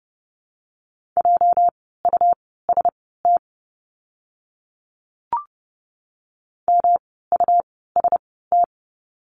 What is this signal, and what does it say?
Morse code, heard as a keyed tone, about 30 wpm, spaced as for 17 wpm, 700 Hz. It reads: JUST MUST
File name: course / 30:10wpm / 237 Lesson 38f - Letter q - Words 30wpm